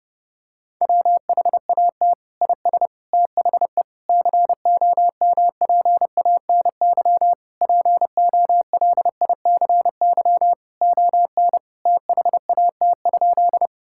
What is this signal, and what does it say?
Morse code, heard as a keyed tone, about 32 wpm, 700 Hz. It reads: WHAT IS THE COMPANY POLICY ON THAT?
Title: Sentences from Top 1000 Words - Review Part 5 - 30wpm